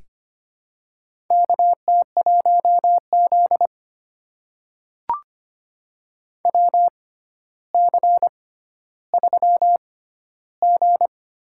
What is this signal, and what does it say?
Morse code, heard as a keyed tone, about 25 wpm, spaced as for 13 wpm, 700 Hz. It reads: KT1Z WC3G
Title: USA Call Signs - 4 Characters - 6x Character Spacing - 25wpm